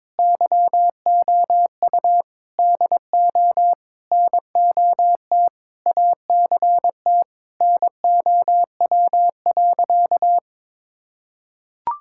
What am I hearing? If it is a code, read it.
Morse code, heard as a keyed tone, about 22 wpm, 700 Hz. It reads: YOU DO NOT ACT NOW.
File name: practice / 22wpm / Sentences with 8-Word Phrases - 22wpm